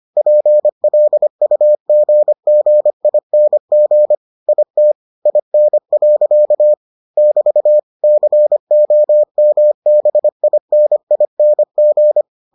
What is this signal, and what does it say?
Morse code, heard as a keyed tone, about 25 wpm, 600 Hz. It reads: PLUGGING IT IN. <BT> COMBINING